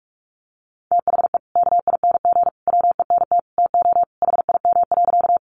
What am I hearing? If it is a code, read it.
Morse code, heard as a keyed tone, about 44 wpm, 700 Hz. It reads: THE KING WENT TO HIM.
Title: Sentences from Top 400 Words 45wpm